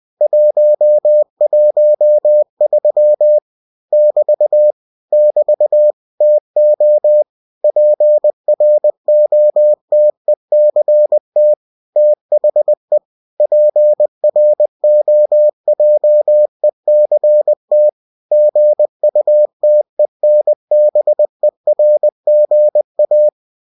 Morse code, heard as a keyed tone, about 20 wpm, 600 Hz. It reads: 113 <BT> <BT> TO PROTECT THE PROJECT GUTENBERGA
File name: cw-books / winnie-the-pooh / winnie-20wpm-page-0113